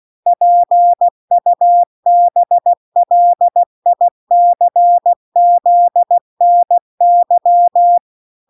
Morse code, heard as a keyed tone, about 16 wpm, 700 Hz. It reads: PUBLICZNY